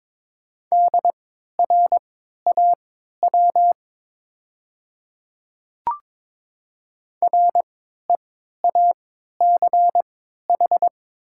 Morse code, heard as a keyed tone, about 22 wpm, spaced as for 12 wpm, 700 Hz. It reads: DRAW REACH
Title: 3x Character Spacing - Top 500 Words - 22wpm